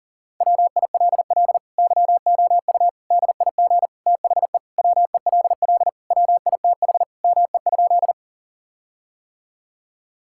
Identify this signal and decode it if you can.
Morse code, heard as a keyed tone, about 40 wpm, 700 Hz. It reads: WILL YOU DIG THE WELL WITH ME?
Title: Sentences from Top 1600 Words 40wpm